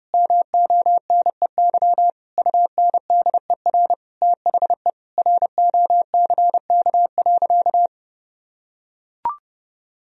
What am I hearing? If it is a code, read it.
Morse code, heard as a keyed tone, about 30 wpm, 700 Hz. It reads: MONEY UNDER THE ROCK.
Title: Sentences from Top 1000 Words - Review Part 2 - 30wpm